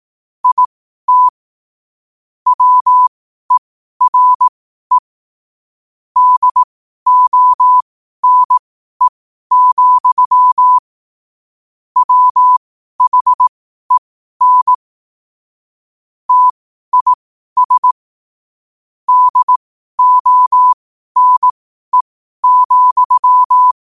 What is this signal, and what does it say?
Morse code, heard as a keyed tone, about 18 wpm, spaced as for 12 wpm, 1000 Hz. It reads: IT WERE DONE, WHEN TIS DONE,
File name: qotdfiles / 12wpm